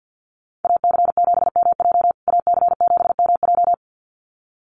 Morse code, heard as a keyed tone, about 48 wpm, 700 Hz. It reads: AC7MJ AC7MJ
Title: USA Call Signs Repeated 2x 50wpm